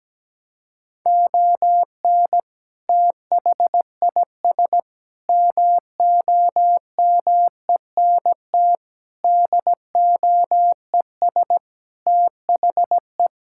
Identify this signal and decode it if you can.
Morse code, heard as a keyed tone, about 17 wpm, 700 Hz. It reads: ON THIS MOMENT DOES THE
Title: Sentences from Top 1600 Words 17wpm